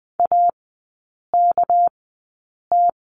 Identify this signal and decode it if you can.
Morse code, heard as a keyed tone, about 20 wpm, spaced as for 8 wpm, 690 Hz. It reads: AKT
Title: 198 Lesson 32e - Letter k - Three Characters 20wpm